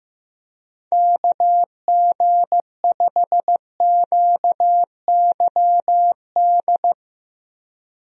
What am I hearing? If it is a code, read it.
Morse code, heard as a keyed tone, about 15 wpm, 700 Hz. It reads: KG5QYD
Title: USA Call Signs - 6 Characters - Repeated 3x - 15wpm